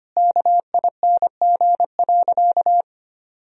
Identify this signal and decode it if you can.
Morse code, heard as a keyed tone, about 25 wpm, 700 Hz. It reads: KING.
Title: Sentences from Top 1000 Words - Review Part 3 - 25wpm